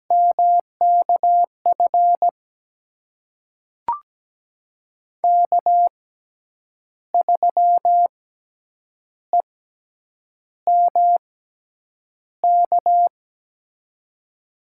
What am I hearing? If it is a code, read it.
Morse code, heard as a keyed tone, about 17 wpm, spaced as for 7 wpm, 700 Hz. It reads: MKF K3EMK
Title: USA Call Signs - 5 Characters - 6x Character Spacing - 17wpm